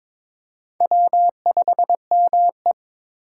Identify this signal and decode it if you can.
Morse code, heard as a keyed tone, about 22 wpm, 700 Hz. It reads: W5ME
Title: USA Call Signs - 8x Character Spacing - 22wpm